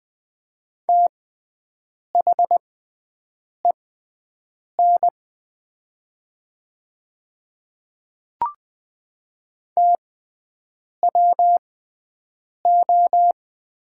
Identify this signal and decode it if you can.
Morse code, heard as a keyed tone, about 20 wpm, spaced as for 7 wpm, 700 Hz. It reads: THEN TWO
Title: 6x Character Spacing - Top 200 Words - 20wpm